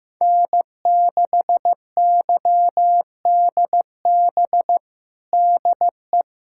Morse code, heard as a keyed tone, about 15 wpm, 700 Hz. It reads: N6YDB DE
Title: USA Callsign DE Callsign - 5 Characters - 15wpm